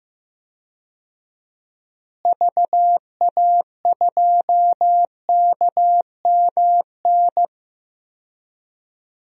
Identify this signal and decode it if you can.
Morse code, heard as a keyed tone, about 15 wpm, 700 Hz. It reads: VA2KMN
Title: DXCC-100 Call Signs - 6 Characters - Repeated 3x - 15wpm